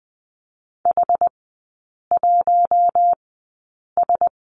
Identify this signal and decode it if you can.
Morse code, heard as a keyed tone, about 20 wpm, spaced as for 8 wpm, 700 Hz. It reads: H1S
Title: 067 Lesson 12e - Letter d - Three Characters 20wpm